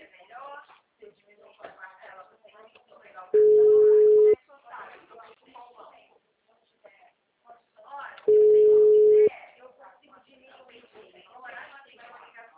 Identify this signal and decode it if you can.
Ringback tone (Europe, 425 Hz), with speech.
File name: out- -2024- - - .20050